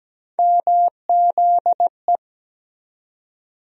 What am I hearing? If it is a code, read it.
Morse code, heard as a keyed tone, about 17 wpm, 700 Hz. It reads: MZE
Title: USA Call Sign Postfixes - 3 Letters 17wpm